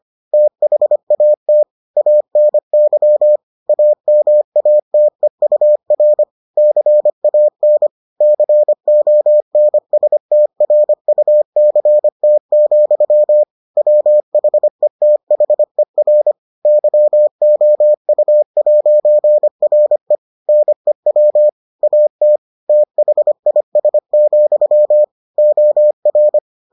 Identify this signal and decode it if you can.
Morse code, heard as a keyed tone, about 25 wpm, 600 Hz. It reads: THAT ANY AMATEUR CAN CONSTRUCT, WHETHER YOU'RE NEW AT THIS, OR